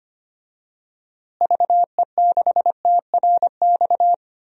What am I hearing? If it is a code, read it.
Morse code, heard as a keyed tone, about 25 wpm, 700 Hz. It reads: VE6TRX
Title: DXCC-100 Call Signs - 6 Characters - Repeated 2x - 25wpm